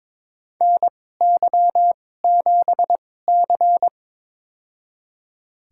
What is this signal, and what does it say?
Morse code, heard as a keyed tone, about 22 wpm, spaced as for 16 wpm, 700 Hz. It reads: NY7C
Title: USA Call Signs - 4 Characters - 2x Character Spacing - 22wpm